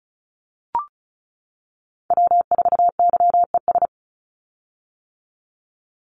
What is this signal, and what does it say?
Morse code, heard as a keyed tone, about 34 wpm, 700 Hz. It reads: W4YES
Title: Call Signs - 35wpm